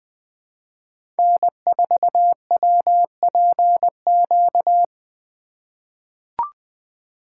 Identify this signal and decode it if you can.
Morse code, heard as a keyed tone, about 20 wpm, 700 Hz. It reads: N4WPQ
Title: USA Call Signs - 5 Characters - 3x Character Spacing - 20wpm